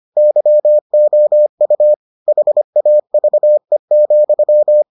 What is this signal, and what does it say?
Morse code, heard as a keyed tone, about 25 wpm, 600 Hz. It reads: YOU HAVE,